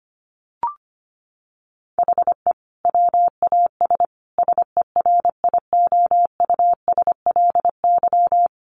Morse code, heard as a keyed tone, about 25 wpm, 700 Hz. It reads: HE WAS SERIOUSLY
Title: Sentences from Top 1200 Words 25wpm